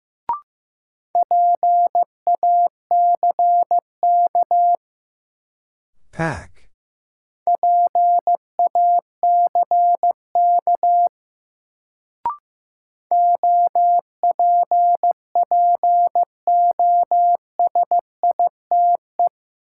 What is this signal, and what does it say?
Morse code, heard as a keyed tone, about 16 wpm, 700 Hz. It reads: PACK PACK OPPOSITE
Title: Sentences from Top 1300 Words - New Word Drill Alt - 15wpm